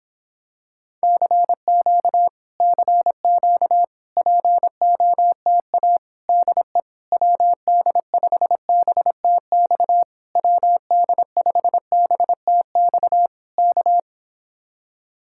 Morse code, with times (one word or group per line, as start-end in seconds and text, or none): 1.0-2.3 CQ
2.6-3.8 CQ
4.2-6.0 POTA
6.3-6.8 DE
7.1-10.0 WD5BTX
10.4-13.3 WD5BTX
13.6-14.0 K